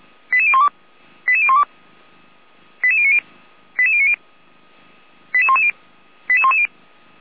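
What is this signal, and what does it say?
Five-tone selective call (ZVEI), heard as quick runs of five tones, tones of about 70 ms. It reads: 88012 88012 88090 88090 88110 88110
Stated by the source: Five-tone ZVEI selective call, address 88447, channel 455, received on an SDR by Feuerwehr Orlamünde.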